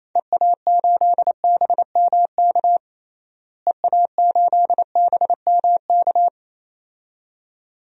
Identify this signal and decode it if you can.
Morse code, heard as a keyed tone, about 27 wpm, 700 Hz. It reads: EA8BMK EA8BMK
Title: DXCC-100 Call Signs - 6 Characters - Repeated 2x - 28wpm